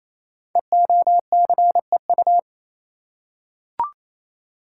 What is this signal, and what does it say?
Morse code, heard as a keyed tone, about 28 wpm, 700 Hz. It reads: EOCEU
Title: Code Groups - 5 Letter:Numbers - 28wpm